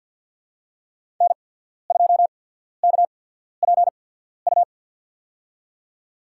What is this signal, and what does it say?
Morse code, heard as a keyed tone, about 53 wpm, spaced as for 14 wpm, 700 Hz. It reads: N2KPU
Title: USA Call Signs - 5 Characters - 8x Character Spacing - 50wpm